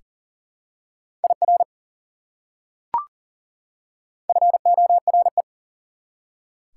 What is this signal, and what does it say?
Morse code, heard as a keyed tone, about 40 wpm, 700 Hz. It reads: IR FORE
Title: Prefixes and Suffixes 40wpm